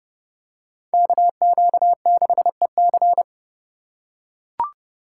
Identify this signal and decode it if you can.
Morse code, heard as a keyed tone, about 28 wpm, 700 Hz. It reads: KQ6EC